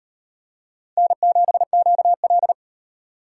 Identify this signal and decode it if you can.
Morse code, heard as a keyed tone, about 38 wpm, 700 Hz. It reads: N7QL